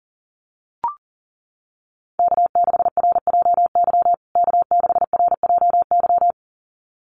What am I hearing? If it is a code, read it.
Morse code, heard as a keyed tone, about 40 wpm, 700 Hz. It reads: K6RJY K6RJY
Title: USA Call Signs Repeated 2x 40wpm